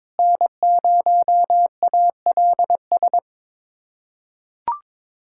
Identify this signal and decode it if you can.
Morse code, heard as a keyed tone, about 22 wpm, 700 Hz. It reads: N0ALS
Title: USA Callsign DE Callsign - 5 Characters - 22wpm